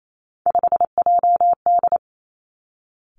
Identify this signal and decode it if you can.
Morse code, heard as a keyed tone, about 28 wpm, 700 Hz. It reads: 5JD
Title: Three Challenging Letter-Number Combinations 28wpm